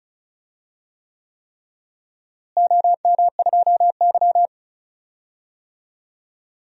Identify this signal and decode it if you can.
Morse code, heard as a keyed tone, about 36 wpm, 700 Hz. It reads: OM2Y
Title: DXCC-100 Call Signs - 4 Characters - Repeated 3x - 35wpm